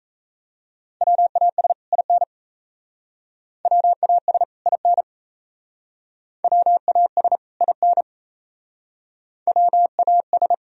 Morse code, heard as a keyed tone, about 34 wpm, 700 Hz. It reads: WAS IN WAS IN WAS IN WAS
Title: Speed-Racing - Sets of 2 Words 28wpm